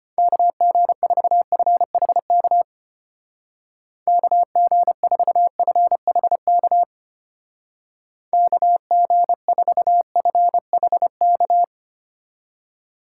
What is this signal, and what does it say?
Morse code, heard as a keyed tone, about 32 wpm, 700 Hz. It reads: KG4FHK KG4FHK KG4FHK